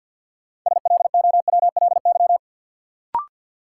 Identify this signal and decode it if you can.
Morse code, heard as a keyed tone, about 52 wpm, 700 Hz. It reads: SLOWLY
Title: Sentences from Top 1100 Words - New Word Drill Alt - 50wpm